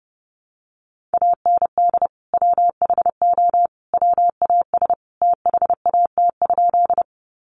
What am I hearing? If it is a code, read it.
Morse code, heard as a keyed tone, about 30 wpm, 700 Hz. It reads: AND WHO WAS THAT?
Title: Sentences from Top 100 Words 30wpm